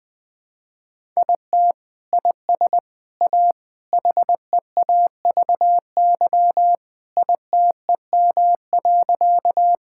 Morse code, heard as a keyed tone, about 19 wpm, 700 Hz. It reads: IT IS A HEAVY ITEM.
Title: Sentences from Top 1000 Words - Review Part 6 - 20wpm